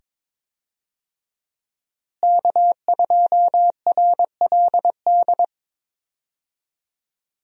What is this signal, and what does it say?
Morse code, heard as a keyed tone, about 22 wpm, 700 Hz. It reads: K2RLD
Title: USA Call Signs - 5 Characters - Repeated 2x - 22wpm